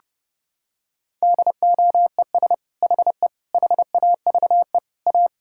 Morse code, heard as a keyed tone, about 30 wpm, 700 Hz. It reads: DOES HE HAVE A
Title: Sentences from Top 1600 Words 30wpm